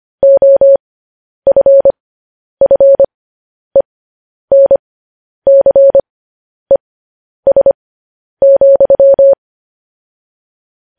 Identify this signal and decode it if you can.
Morse code, heard as a keyed tone, about 25 wpm, spaced as for 10 wpm, 550 Hz. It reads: OFFENCES,
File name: podcast / News headlines-1x-25x10